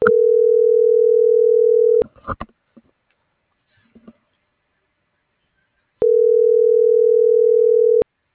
Ringback tone (North America), 440 + 480 Hz.